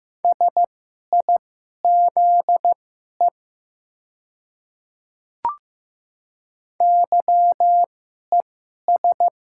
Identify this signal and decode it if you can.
Morse code, heard as a keyed tone, about 15 wpm, spaced as for 11 wpm, 690 Hz. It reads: SIZE YES
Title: 2x Character Spacing - Top 500 Words - 15wpm